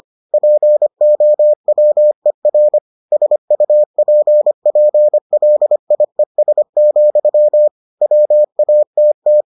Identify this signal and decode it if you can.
Morse code, heard as a keyed tone, about 25 wpm, 600 Hz. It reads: POWER SUPPLIES, WATT